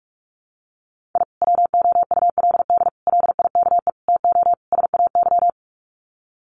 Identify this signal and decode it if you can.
Morse code, heard as a keyed tone, about 44 wpm, 700 Hz. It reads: I WOULD LIKE TO SAY
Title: Sets of 5 Words Encore2 45wpm